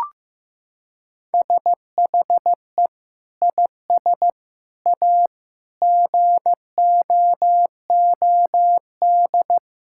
Morse code, heard as a keyed tone, about 15 wpm, 700 Hz. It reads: SHE IS A GOOD
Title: Sentences from Top 1600 Words 15wpm